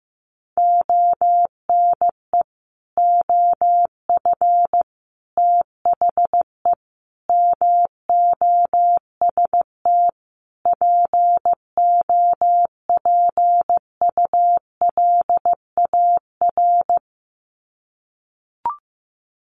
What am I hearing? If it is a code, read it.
Morse code, heard as a keyed tone, about 15 wpm, 700 Hz. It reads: ONE OF THE MOST POPULAR
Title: Sets of 5 Words 15wpm